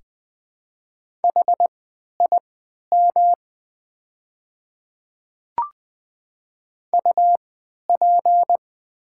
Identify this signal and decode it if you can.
Morse code, heard as a keyed tone, about 20 wpm, spaced as for 11 wpm, 700 Hz. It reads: HIM UP
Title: 3x Character Spacing - Top 200 Words - 20wpm